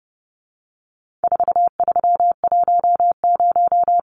Morse code, heard as a keyed tone, about 30 wpm, 700 Hz. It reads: 4310
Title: Numbers - 4-Digit 30wpm